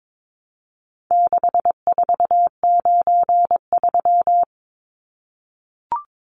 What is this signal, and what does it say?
Morse code, heard as a keyed tone, about 22 wpm, 700 Hz. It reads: 6493
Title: Numbers - 4-Digit 22wpm